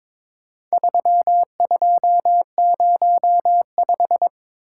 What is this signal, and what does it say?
Morse code, heard as a keyed tone, about 22 wpm, 700 Hz. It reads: 3205
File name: practice / 22wpm / Four Letter-Number Combinations - 22wpm